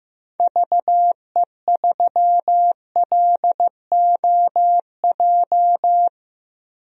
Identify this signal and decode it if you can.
Morse code, heard as a keyed tone, about 15 wpm, 700 Hz. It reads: VE3LOJ
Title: DXCC-100 Call Signs - 6 Characters - Repeated 3x - 15wpm